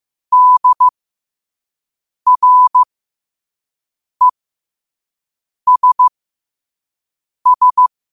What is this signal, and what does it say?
Morse code, heard as a keyed tone, about 15 wpm, spaced as for 5 wpm, 1000 Hz. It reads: DRESS
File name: qotdfiles / 5wpm